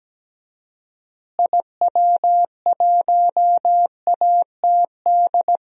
Morse code, heard as a keyed tone, about 17 wpm, 700 Hz. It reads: IW1ATD
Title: DXCC-100 Call Signs - 6 Characters - Repeated 3x - 17wpm